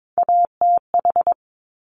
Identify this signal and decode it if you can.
Morse code, heard as a keyed tone, about 22 wpm, 700 Hz. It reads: ATH
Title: Top Three Letter Combinations in Words 22wpm